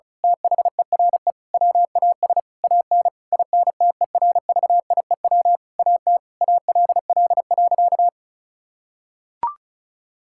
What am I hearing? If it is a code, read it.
Morse code, heard as a keyed tone, about 36 wpm, 700 Hz. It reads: THERE WAS AN INTERVIEW AT ALL.
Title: Sentences from Top 1600 Words 35wpm